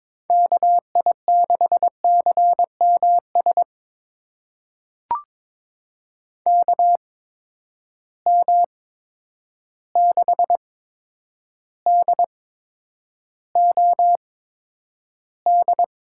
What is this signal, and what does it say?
Morse code, heard as a keyed tone, about 22 wpm, spaced as for 9 wpm, 700 Hz. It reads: KI6CMS KM6DOD
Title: USA Call Signs - 6 Characters - 8x Character Spacing - 22wpm